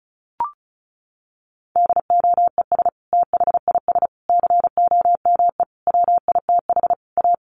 Morse code, heard as a keyed tone, about 34 wpm, 700 Hz. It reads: DOES THIS COME WITH A
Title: Sentences from Top 1400 Words 35wpm